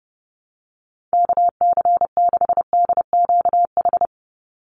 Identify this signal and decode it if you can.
Morse code, heard as a keyed tone, about 30 wpm, 700 Hz. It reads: KC6DQH
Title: USA Call Signs 30wpm